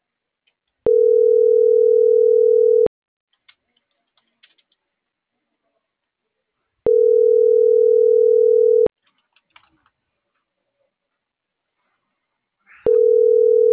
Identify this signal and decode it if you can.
Ringback tone (North America), 440 + 480 Hz.